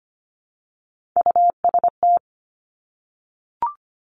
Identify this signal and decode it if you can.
Morse code, heard as a keyed tone, about 25 wpm, 700 Hz. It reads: UST
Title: Top Three Letter Combinations in Words 25wpm